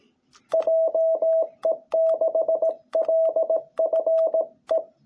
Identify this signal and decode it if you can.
Morse code, heard as a keyed tone, about 18 wpm, 660 Hz. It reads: JE6LFE
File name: jh8cbh / 2021cw